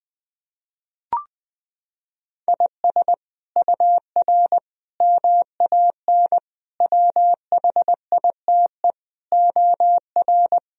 Morse code, heard as a keyed tone, about 20 wpm, 700 Hz. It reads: IS UR MAN WHITE OR
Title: Sentences from Top 400 Words and Abbreviations 20wpm